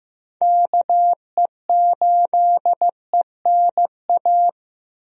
Morse code, heard as a keyed tone, about 15 wpm, 700 Hz. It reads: KE8ENA
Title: USA Call Signs - 6 Characters - Repeated 3x - 15wpm